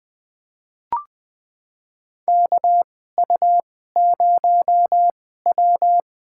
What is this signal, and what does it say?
Morse code, heard as a keyed tone, about 20 wpm, spaced as for 14 wpm, 700 Hz. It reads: KU0W
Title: USA Call Signs - 4 Characters - 2x Character Spacing - 20wpm